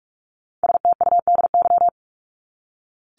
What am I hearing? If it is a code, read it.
Morse code, heard as a keyed tone, about 44 wpm, 700 Hz. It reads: STUDY
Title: ES:S Suffix - Top 100 Words with Base Words 45wpm